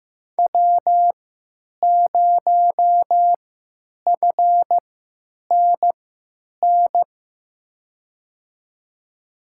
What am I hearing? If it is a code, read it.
Morse code, heard as a keyed tone, about 15 wpm, spaced as for 9 wpm, 700 Hz. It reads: W0FNN
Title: USA Call Signs - 5 Characters - 3x Character Spacing - 15wpm